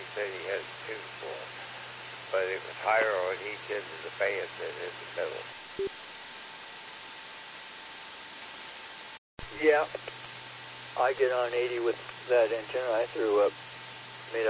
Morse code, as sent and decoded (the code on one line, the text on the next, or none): none